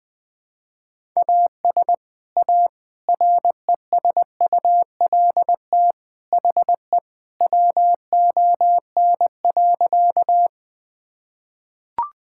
Morse code, heard as a keyed tone, about 20 wpm, 700 Hz. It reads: AS A RESULT HE WON.
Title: Sentences with 3-Word Phrases - 20wpm